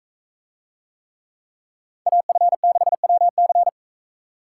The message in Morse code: AF6WC

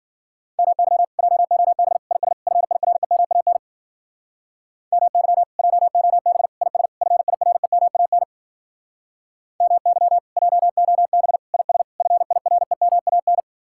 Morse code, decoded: MY JOB IS FIREMAN MY JOB IS FIREMAN MY JOB IS FIREMAN

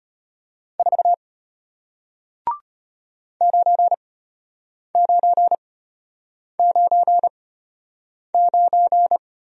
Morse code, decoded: 4 9999

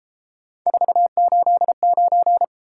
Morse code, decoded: 489

